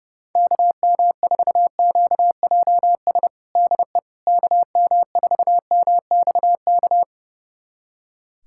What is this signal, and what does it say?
Morse code, heard as a keyed tone, about 32 wpm, 700 Hz. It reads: KM4QJS DE KM4MXK
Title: USA Callsign DE Callsign - 6 Characters - 30wpm